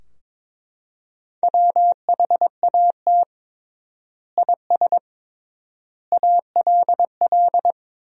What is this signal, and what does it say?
Morse code, heard as a keyed tone, about 22 wpm, 700 Hz. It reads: WHAT IS ALL
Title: 3x Word-Spacing - Sentences from Top 100 Words - 22wpm